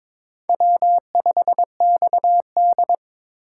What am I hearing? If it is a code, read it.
Morse code, heard as a keyed tone, about 26 wpm, 700 Hz. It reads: W5XD
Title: Speed Racing - USA Call Signs 22wpm